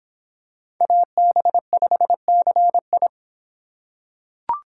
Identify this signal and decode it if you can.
Morse code, heard as a keyed tone, about 26 wpm, 700 Hz. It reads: AB5CI